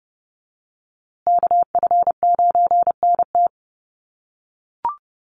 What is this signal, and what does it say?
Morse code, heard as a keyed tone, about 30 wpm, 700 Hz. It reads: KF9NT
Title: USA Call Signs Repeated 2x 30wpm